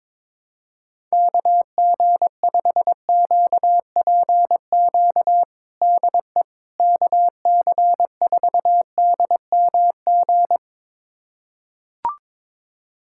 Morse code, heard as a keyed tone, about 22 wpm, 700 Hz. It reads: KG5QPQ DE KC4DMG
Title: USA Callsign DE Callsign - 6 Characters - 22wpm